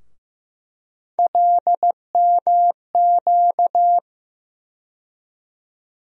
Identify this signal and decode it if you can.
Morse code, heard as a keyed tone, about 15 wpm, 700 Hz. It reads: LMQ
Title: USA Call Sign Postfixes - 3 Letters 15wpm